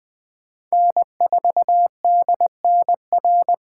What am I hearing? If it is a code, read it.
Morse code, heard as a keyed tone, about 20 wpm, 700 Hz. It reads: N4DNR
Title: Speed Racing - USA 1x3 Call Signs 17wpm